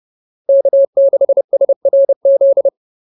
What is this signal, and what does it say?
Morse code, heard as a keyed tone, about 30 wpm, 590 Hz. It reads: K6SRZ
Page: advanced-practice-files